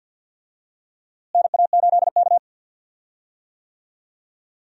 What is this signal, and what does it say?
Morse code, heard as a keyed tone, about 47 wpm, 700 Hz. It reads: NA8K
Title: USA Call Signs - 4 Characters - Repeated 3x - 50wpm